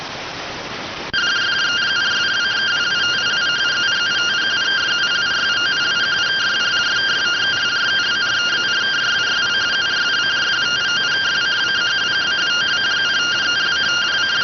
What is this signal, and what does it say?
Two-tone teleprinter signal, about 170 Hz shift.